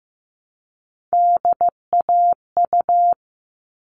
Morse code, heard as a keyed tone, about 15 wpm, 700 Hz. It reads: DAU
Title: All Cut Numbers - 3-Digit 15wpm